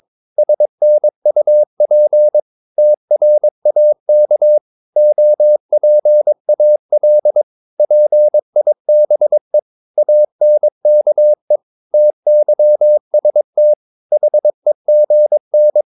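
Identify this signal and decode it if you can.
Morse code, heard as a keyed tone, about 22 wpm, 600 Hz. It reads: SNUP TRAK OPAL PIBE ANKE TYST HEGN
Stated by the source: Morse practice of short Danish words at 22 wpm, from CW med Gnister.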